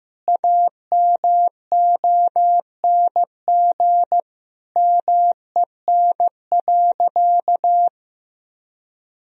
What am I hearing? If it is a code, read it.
Morse code, heard as a keyed tone, about 15 wpm, 700 Hz. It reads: AMONG MEN.
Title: Sentences from Top 1000 Words - Review Part 3 - 15wpm